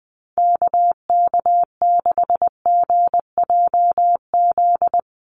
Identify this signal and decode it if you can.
Morse code, heard as a keyed tone, about 20 wpm, 700 Hz. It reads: KK6GJZ
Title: USA Call Signs Repeated 3x 20wpm